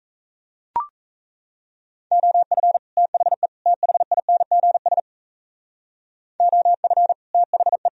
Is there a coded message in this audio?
Yes: Morse code from 2.1 to 7.9 s, heard as a keyed tone, about 36 wpm, 700 Hz. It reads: OF THE THINGS OF THE